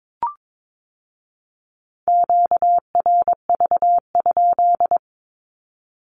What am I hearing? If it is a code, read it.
Morse code, heard as a keyed tone, about 22 wpm, 700 Hz. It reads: QRV?
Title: Q-Codes with Spoken Meaning - 22wpm